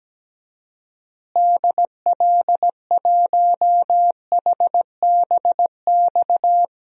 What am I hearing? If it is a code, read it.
Morse code, heard as a keyed tone, about 17 wpm, 700 Hz. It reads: DL1HBX
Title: DXCC-100 Call Signs - 6 Characters - Repeated 2x - 17wpm